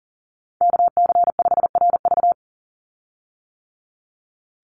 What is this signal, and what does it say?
Morse code, heard as a keyed tone, about 40 wpm, 700 Hz. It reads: KC5RV
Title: USA Call Signs 40wpm